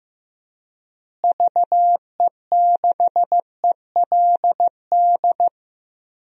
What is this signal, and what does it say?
Morse code, heard as a keyed tone, about 15 wpm, 700 Hz. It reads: VE6ELD
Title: DXCC-100 Call Signs - 6 Characters - Repeated 3x - 15wpm